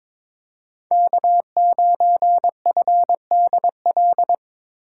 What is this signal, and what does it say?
Morse code, heard as a keyed tone, about 26 wpm, 700 Hz. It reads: K9FDL
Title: Speed Racing - USA 1x3 Call Signs 22wpm